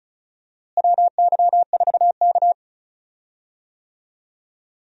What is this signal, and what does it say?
Morse code, heard as a keyed tone, about 34 wpm, 700 Hz. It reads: WY4K
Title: USA Call Signs - 4 Characters - Repeated 2x - 35wpm